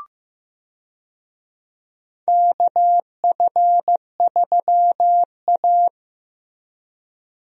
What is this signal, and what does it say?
Morse code, heard as a keyed tone, about 15 wpm, 700 Hz. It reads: KF3A
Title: USA Call Signs - 4 Characters - Repeated 2x - 15wpm